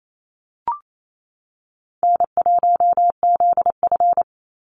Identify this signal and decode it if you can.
Morse code, heard as a keyed tone, about 28 wpm, 700 Hz. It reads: N1ZF
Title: NA SOTA Activator Callsigns 28wpm